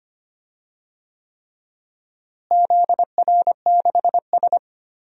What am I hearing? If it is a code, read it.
Morse code, heard as a keyed tone, about 25 wpm, 700 Hz. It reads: ZR6S
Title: DXCC-100 Call Signs - 4 Characters - Repeated 3x - 25wpm